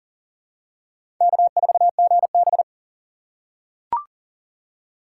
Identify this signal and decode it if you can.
Morse code, heard as a keyed tone, about 40 wpm, 700 Hz. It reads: K4GB
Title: USA Call Signs - 4 Characters - 6x Character Spacing - 40wpm